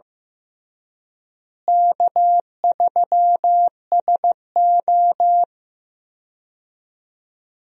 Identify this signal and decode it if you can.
Morse code, heard as a keyed tone, about 15 wpm, 700 Hz. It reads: K3SO
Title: USA Call Signs - 4 Characters - Repeated 3x - 15wpm